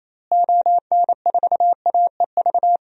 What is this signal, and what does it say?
Morse code, heard as a keyed tone, about 28 wpm, 700 Hz. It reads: ON4AEV
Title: DXCC-100 Call Signs - 6 Characters - Repeated 3x - 28wpm